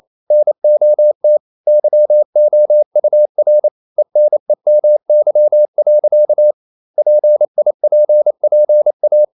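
Morse code, heard as a keyed tone, about 28 wpm, 600 Hz. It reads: NOT YOUR ENEMY. PIPPA
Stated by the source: Morse code practice at 28 wpm, text of Ken Follett's novel Never in English, from CW med Gnister.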